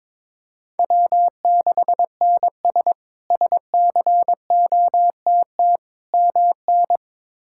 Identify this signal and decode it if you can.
Morse code, heard as a keyed tone, about 22 wpm, 700 Hz. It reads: W6NS SCOTT MN
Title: K1USN SST QSO Practice - 22wpm